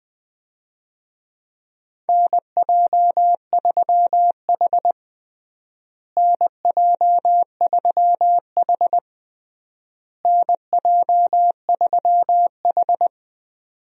Morse code, heard as a keyed tone, about 20 wpm, 700 Hz. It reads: NJ3H NJ3H NJ3H
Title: USA Call Signs - 4 Characters - Repeated 3x - 20wpm